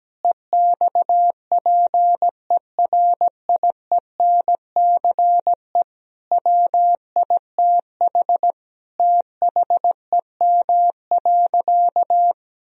Morse code, heard as a keyed tone, about 17 wpm, 700 Hz. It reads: EXPERIENCE WITH THEM.
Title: Sentences from Top 1000 Words - Review Part 3 - 17wpm